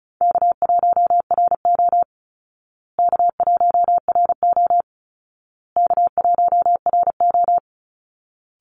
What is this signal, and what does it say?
Morse code, heard as a keyed tone, about 34 wpm, 700 Hz. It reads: K1RO K1RO K1RO
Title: Easy Call Signs Repeated 3x - 35wpm